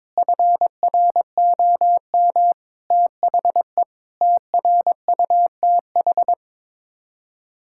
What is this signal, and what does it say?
Morse code, heard as a keyed tone, about 22 wpm, 700 Hz. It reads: FROM THE TRUTH
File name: practice / 22wpm / Sets of 7 Words - 22wpm